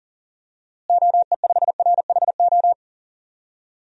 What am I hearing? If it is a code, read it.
Morse code, heard as a keyed tone, about 40 wpm, 700 Hz. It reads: OE5RHO
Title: DXCC-100 Call Signs - 6 Characters - Repeated 3x - 40wpm